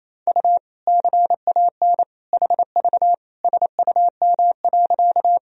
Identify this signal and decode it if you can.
Morse code, heard as a keyed tone, about 28 wpm, 700 Hz. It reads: U CAN HV SUM.
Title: Sentences from Top 100 Words and Abbreviations 28wpm